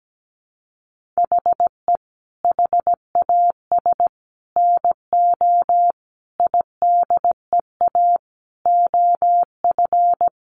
Morse code, heard as a keyed tone, about 17 wpm, 700 Hz. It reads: HE HAS NO IDEA OF
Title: Sentences from Top 1400 Words 17wpm